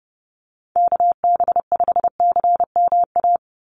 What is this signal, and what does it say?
Morse code, heard as a keyed tone, about 30 wpm, 700 Hz. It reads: KB5CMA
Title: USA Call Signs 30wpm